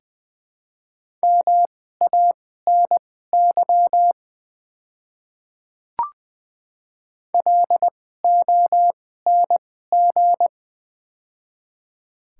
Morse code, heard as a keyed tone, about 20 wpm, spaced as for 15 wpm, 700 Hz. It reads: MANY LONG
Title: 2x Character Spacing - Top 100 Words - 20wpm